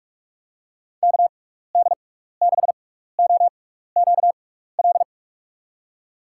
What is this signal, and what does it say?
Morse code, heard as a keyed tone, about 46 wpm, spaced as for 15 wpm, 700 Hz. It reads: KD6OQL
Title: USA Call Signs - 6x Character Spacing - 45wpm